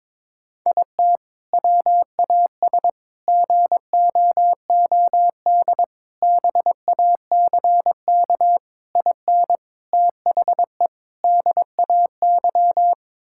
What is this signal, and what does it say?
Morse code, heard as a keyed tone, about 22 wpm, 700 Hz. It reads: IT WAS GOOD BACK IN THE DAY
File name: practice / 22wpm / Sentences with 3-Word Phrases - 22wpm